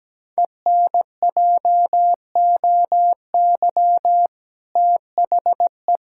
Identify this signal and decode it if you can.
Morse code, heard as a keyed tone, about 17 wpm, 700 Hz. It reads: ENJOY THE
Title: Sentences from Top 1000 Words - Review Part 5 - 17wpm